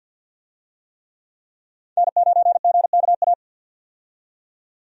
Morse code, heard as a keyed tone, about 53 wpm, 700 Hz. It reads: N9GKA
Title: USA Call Signs - 5 Characters - 50wpm